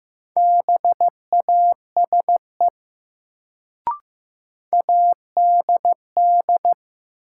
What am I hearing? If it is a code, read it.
Morse code, heard as a keyed tone, about 15 wpm, 700 Hz. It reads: BASE ADD
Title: Sentences from Top 1000 Words - Review Part 1 - New Word Drill - 15wpm